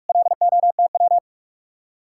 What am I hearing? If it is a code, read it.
Morse code, heard as a keyed tone, about 56 wpm, 700 Hz. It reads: LOTW
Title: Speed-Racing Common QSO Phrases - 45wpm